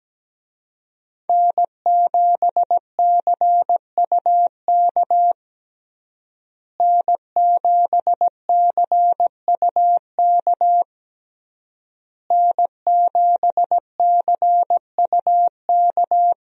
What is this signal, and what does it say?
Morse code, heard as a keyed tone, about 17 wpm, 700 Hz. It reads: N7CUK N7CUK N7CUK
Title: USA Call Signs - 5 Characters - Repeated 3x - 17wpm